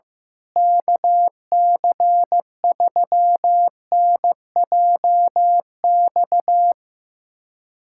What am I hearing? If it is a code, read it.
Morse code, heard as a keyed tone, about 16 wpm, 700 Hz. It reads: KC3NJX
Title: USA Callsign DE Callsign - 6 Characters - 15wpm